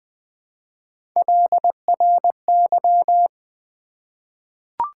Morse code, heard as a keyed tone, about 20 wpm, 700 Hz. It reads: LRY
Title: Three Letter-Number Combinations - 20wpm